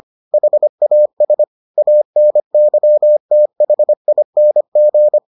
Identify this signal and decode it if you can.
Morse code, heard as a keyed tone, about 25 wpm, 600 Hz. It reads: HAS ANYTHING